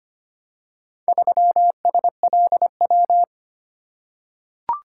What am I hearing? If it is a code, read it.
Morse code, heard as a keyed tone, about 25 wpm, 700 Hz. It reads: 3SLW
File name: practice / 25wpm / Four Letter-Number Combinations - 25wpm